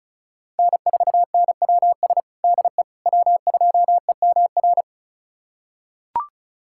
Morse code, heard as a keyed tone, about 36 wpm, 700 Hz. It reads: N4NWS DE W2EMR
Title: USA Callsign DE Callsign - 5 Characters - 35wpm